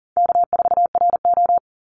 Morse code, heard as a keyed tone, about 40 wpm, 700 Hz. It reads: K4RO
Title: Easy Call Signs - 40wpm